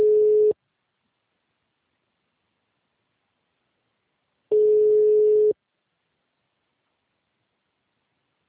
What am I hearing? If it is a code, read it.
Ringback tone (Europe, 425 Hz).